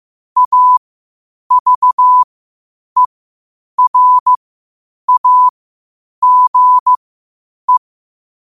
Morse code, heard as a keyed tone, about 15 wpm, spaced as for 8 wpm, 1000 Hz. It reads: AVERAGE
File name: qotdfiles / 8wpm